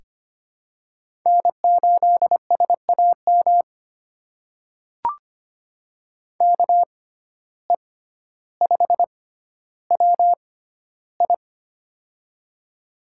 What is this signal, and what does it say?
Morse code, heard as a keyed tone, about 25 wpm, spaced as for 13 wpm, 700 Hz. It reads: N8SAM KE5WI
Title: USA Call Signs - 5 Characters - 6x Character Spacing - 25wpm